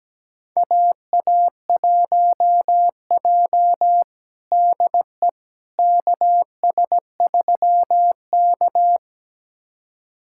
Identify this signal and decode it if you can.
Morse code, heard as a keyed tone, about 17 wpm, 700 Hz. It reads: AA1J DE KS3K
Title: USA Callsign DE Callsign - 4 Characters - 17wpm